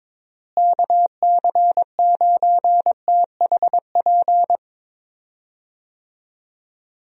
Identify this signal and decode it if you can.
Morse code, heard as a keyed tone, about 22 wpm, 700 Hz. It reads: KC9THP
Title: USA Call Signs - 6 Characters - Repeated 2x - 22wpm